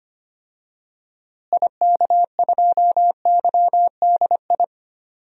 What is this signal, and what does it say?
Morse code, heard as a keyed tone, about 25 wpm, 700 Hz. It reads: IK2YDI
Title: DXCC-100 Call Signs - 6 Characters - Repeated 3x - 25wpm